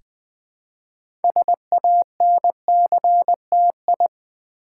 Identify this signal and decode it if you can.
Morse code, heard as a keyed tone, about 20 wpm, 700 Hz. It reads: SANCTI
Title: Build A Word - 3 Letters At A Time - 8-Letter Words - 20wpm